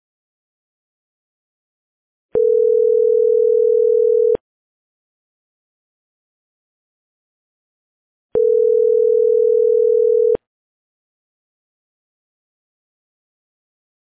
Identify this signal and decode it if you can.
Ringback tone (North America), 440 + 480 Hz.